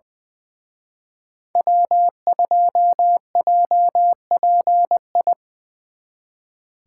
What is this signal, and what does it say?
Morse code, heard as a keyed tone, about 23 wpm, 700 Hz. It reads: W2JPI
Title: Speed Racing - USA 1x3 Call Signs 20wpm